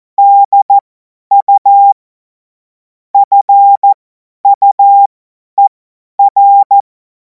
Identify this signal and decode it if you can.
Morse code, heard as a keyed tone, about 14 wpm, spaced as for 10 wpm, 800 Hz. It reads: DU FUER
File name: Morsekurs / Witze / Witz 30. Juni 2026 50 F70